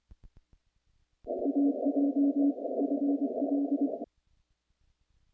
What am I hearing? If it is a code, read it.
Morse code, heard as a keyed tone, about 23 wpm, 280 Hz. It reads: AJ FL
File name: contests / naqp-cw-k5tr-op-k5pi- -0000